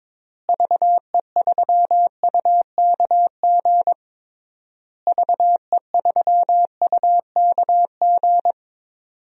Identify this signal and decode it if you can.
Morse code, heard as a keyed tone, about 22 wpm, 700 Hz. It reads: VE3UKG VE3UKG
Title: DXCC-100 Call Signs - 6 Characters - Repeated 3x - 22wpm